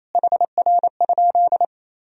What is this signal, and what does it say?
Morse code, heard as a keyed tone, about 28 wpm, 700 Hz. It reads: HR?
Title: Sentences from Top 200 Words and Abbreviations 28wpm